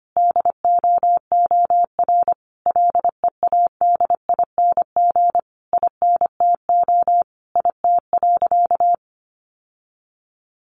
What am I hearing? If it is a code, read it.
Morse code, heard as a keyed tone, about 25 wpm, 700 Hz. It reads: DOOR LEADING INTO IT.
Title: Sentences from Top 1100 Words 25wpm